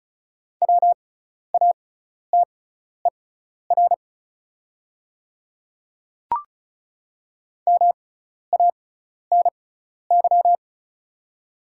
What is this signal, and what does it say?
Morse code, heard as a keyed tone, about 36 wpm, spaced as for 12 wpm, 700 Hz. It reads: WATER MANY
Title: 6x Character Spacing - Top 100 Words - 35wpm